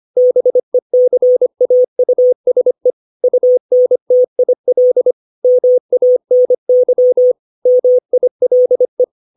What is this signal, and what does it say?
Morse code, heard as a keyed tone, about 25 wpm, 500 Hz. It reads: BECAUSE UNTIL MANY MILE
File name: 632 words / 632 Words 25WPM